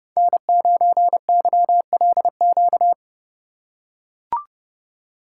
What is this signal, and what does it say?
Morse code, heard as a keyed tone, about 29 wpm, 700 Hz. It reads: N9YLQ
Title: USA Call Signs - 5 Characters - 2x Character Spacing - 30wpm